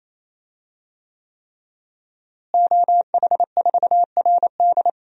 Morse code, heard as a keyed tone, about 28 wpm, 700 Hz. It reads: OH4RD
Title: DXCC-100 Call Signs - 5 Characters - Repeated 3x - 28wpm